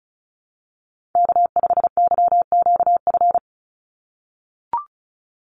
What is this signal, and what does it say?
Morse code, heard as a keyed tone, about 36 wpm, 700 Hz. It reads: K5YQF
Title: Call Signs - 35wpm